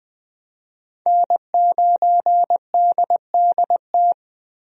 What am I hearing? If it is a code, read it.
Morse code, heard as a keyed tone, about 20 wpm, 700 Hz. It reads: N9DDT
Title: USA Call Signs - 5 Characters - Repeated 3x - 20wpm